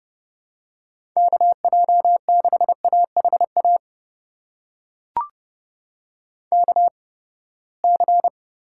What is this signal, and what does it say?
Morse code, heard as a keyed tone, about 30 wpm, spaced as for 21 wpm, 700 Hz. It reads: KJ6AHA KC
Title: USA Call Signs - 6 Characters - 8x Character Spacing - 30wpm